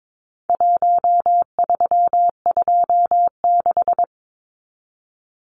Morse code, heard as a keyed tone, about 22 wpm, 700 Hz. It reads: 1326
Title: Numbers - Contest Style - 4-Digit 22wpm